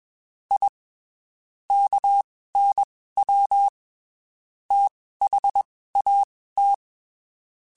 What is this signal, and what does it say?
Morse code, heard as a keyed tone, about 21 wpm, spaced as for 15 wpm, 800 Hz. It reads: I KNW THAT
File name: (spaced) / QSOs Spaced-mp3s / WD0FBT @20 sp